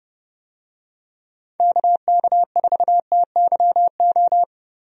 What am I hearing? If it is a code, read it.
Morse code, heard as a keyed tone, about 31 wpm, 700 Hz. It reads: KK4TYO
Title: USA Call Signs - 6 Characters - Repeated 3x - 30wpm